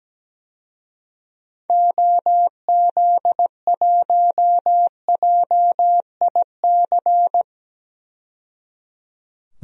Morse code, heard as a keyed tone, about 17 wpm, 700 Hz. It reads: OZ1JIC
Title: DXCC-100 Call Signs - 6 Characters - 17wpm